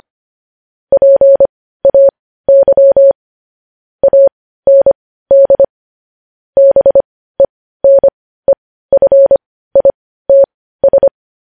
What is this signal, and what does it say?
Morse code, heard as a keyed tone, about 25 wpm, spaced as for 15 wpm, 550 Hz. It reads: PAY AND BENEFITS
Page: News Headlines in Morse Code at 15 WPM